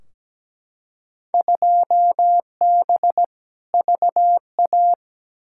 Morse code, heard as a keyed tone, about 17 wpm, 700 Hz. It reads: 2B VA
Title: Field Day Exchanges Encore - 17wpm